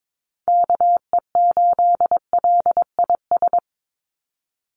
Morse code, heard as a keyed tone, about 22 wpm, 700 Hz. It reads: KE8LIS
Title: USA Call Signs 22wpm